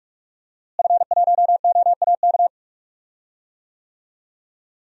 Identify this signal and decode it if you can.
Morse code, heard as a keyed tone, about 46 wpm, 700 Hz. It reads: F1OAK
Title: DXCC-100 Call Signs - 5 Characters - Repeated 3x - 45wpm